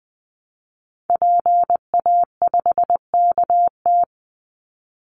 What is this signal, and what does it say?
Morse code, heard as a keyed tone, about 20 wpm, 700 Hz. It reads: PA5KT
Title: Call Signs Repeated 3x - 20wpm